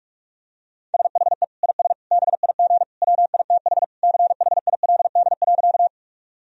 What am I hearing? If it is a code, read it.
Morse code, heard as a keyed tone, about 46 wpm, 700 Hz. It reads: SHE IS BIG WITH CHILD.
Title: Sentences from Top 1000 Words - Review Part 1 - 45wpm